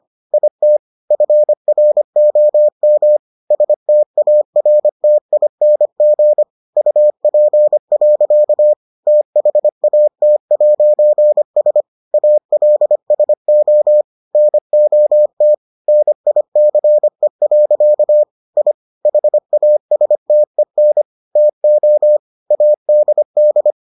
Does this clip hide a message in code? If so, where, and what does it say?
Yes: Morse code from 0.3 to 23.7 s, heard as a keyed tone, about 25 wpm, 600 Hz. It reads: IT FROM STARTING UP. THAT'S ALSO NOT NICE. I HASTEN TO ADD